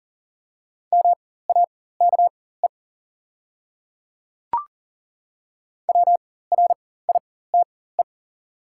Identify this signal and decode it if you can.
Morse code, heard as a keyed tone, about 37 wpm, spaced as for 18 wpm, 690 Hz. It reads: MAKE WRITE
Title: 4x Character Spacing - Top 100 Words - 40wpm